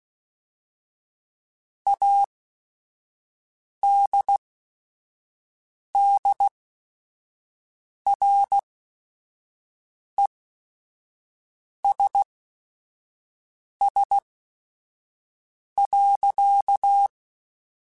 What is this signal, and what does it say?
Morse code, heard as a keyed tone, about 16 wpm, spaced as for 5 wpm, 800 Hz. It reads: ADDRESS.
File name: Chuckles / Chuckles-mp3s / Temptation @5